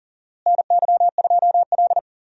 Morse code, heard as a keyed tone, about 37 wpm, 700 Hz. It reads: NY2L
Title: USA Call Signs - 4 Characters - Repeated 3x - 40wpm